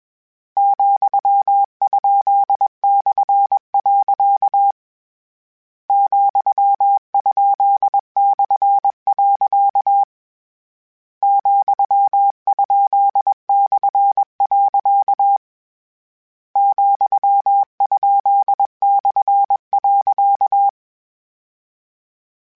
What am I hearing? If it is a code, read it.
Morse code, heard as a keyed tone, about 21 wpm, 800 Hz. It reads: ,?/. ,?/. ,?/. ,?/.
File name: Misc / Misc-mp3s / Punct @20